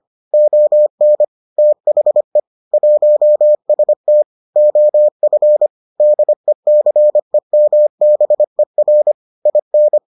Morse code, heard as a keyed tone, about 25 wpm, 600 Hz. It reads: ON THE 1ST OF DECEMBER IN